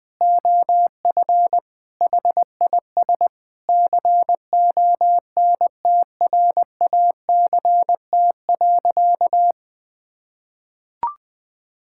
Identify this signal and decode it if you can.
Morse code, heard as a keyed tone, about 20 wpm, 700 Hz. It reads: OF HIS CONTRACT.
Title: Sentences with 6-Word Phrases - 20wpm